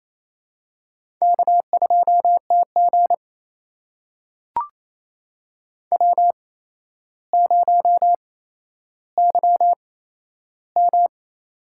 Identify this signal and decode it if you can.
Morse code, heard as a keyed tone, about 27 wpm, spaced as for 12 wpm, 700 Hz. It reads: K2TG W0YM
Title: USA Call Signs - 4 Characters - 8x Character Spacing - 28wpm